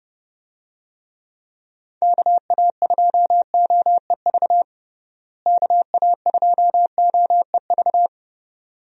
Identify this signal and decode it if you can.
Morse code, heard as a keyed tone, about 30 wpm, 700 Hz. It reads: KA2OEV KA2OEV
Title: USA Call Signs - 6 Characters - Repeated 2x - 30wpm